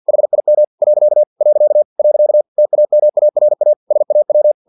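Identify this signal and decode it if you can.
Morse code, heard as a keyed tone, about 50 wpm, 600 Hz. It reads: HIM . . TAMARA SAW
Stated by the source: Morse practice text from an English-language novel at 49 wpm.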